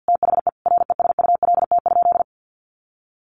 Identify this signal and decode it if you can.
Morse code, heard as a keyed tone, about 50 wpm, 700 Hz. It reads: THE RESULT?
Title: Sentences from Top 1500 Words 50wpm